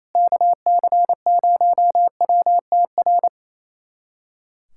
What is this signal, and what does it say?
Morse code, heard as a keyed tone, about 28 wpm, 700 Hz. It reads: KC0WTR
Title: USA Callsign DE Callsign 28wpm